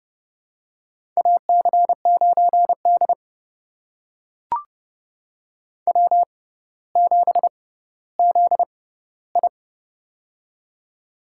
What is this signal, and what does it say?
Morse code, heard as a keyed tone, about 30 wpm, spaced as for 15 wpm, 690 Hz. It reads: AC9D W7ZI
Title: USA Call Signs - 4 Characters - 6x Character Spacing - 30wpm